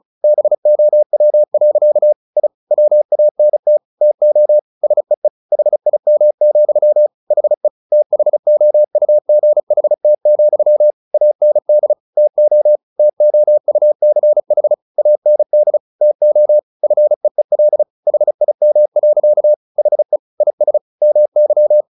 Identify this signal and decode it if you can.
Morse code, heard as a keyed tone, about 34 wpm, 600 Hz. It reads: BOW. I WANT TO SEE HIM, HE THOUGHT, AND TO TOUCH AND TO FEEL HIM. HE IS MY
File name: the old man and the sea 2 35wpm